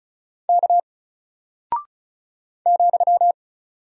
Morse code, heard as a keyed tone, about 34 wpm, 700 Hz. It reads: K ,
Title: Single Character (Letter:Number:Punctuation:Prosign) Alt 35wpm